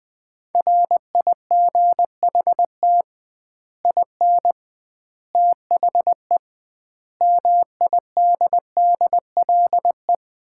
Morse code, heard as a keyed tone, about 20 wpm, 700 Hz. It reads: RIGHT IN THE MIDDLE
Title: 2x Word-Spacing - Sets of 5 Words - 20wpm